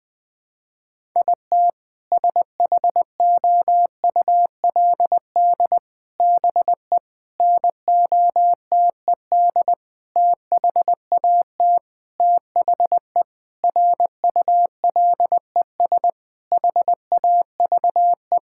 Morse code, heard as a keyed tone, about 20 wpm, 700 Hz. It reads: IT SHOULD BE NOTED THAT THE RULES HAVE
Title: Sentences with 5-Word Phrases - 20wpm